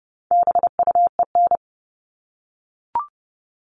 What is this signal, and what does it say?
Morse code, heard as a keyed tone, about 30 wpm, 700 Hz. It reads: BUEN
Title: All Cut Numbers - 4-Digit 30wpm